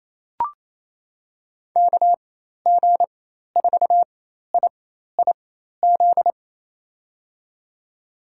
Morse code, heard as a keyed tone, about 28 wpm, spaced as for 13 wpm, 700 Hz. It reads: KG4IIZ
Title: USA Call Signs - 6 Characters - 4x Character Spacing - 28wpm